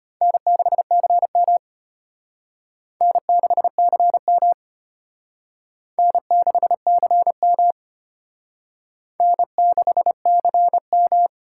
Morse code, heard as a keyed tone, about 32 wpm, 700 Hz. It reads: N6CM N6CM N6CM N6CM